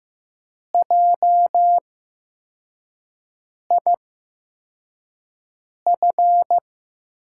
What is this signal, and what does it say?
Morse code, heard as a keyed tone, about 15 wpm, spaced as for 4 wpm, 700 Hz. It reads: JIF
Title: USA Call Signs - 6 Characters - 8x Character Spacing - 15wpm